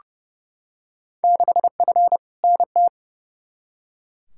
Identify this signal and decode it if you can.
Morse code, heard as a keyed tone, about 30 wpm, 700 Hz. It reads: 6F NT
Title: Field Day Exchanges Encore - 30wpm